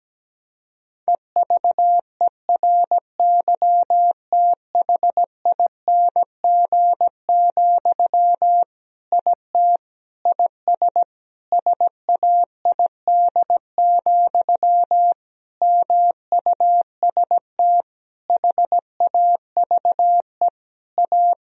Morse code, heard as a keyed tone, about 17 wpm, 700 Hz. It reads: EVERYTHING, IT IS SAID, MUST HAVE A
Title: Sentences from Top 1000 Words - Review Part 1 - 17wpm